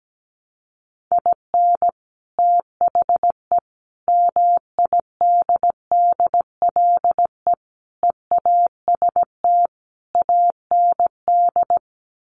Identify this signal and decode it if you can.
Morse code, heard as a keyed tone, about 17 wpm, 690 Hz. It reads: IN THE MIDDLE EAST AND
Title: Sets of 5 Words Encore 17wpm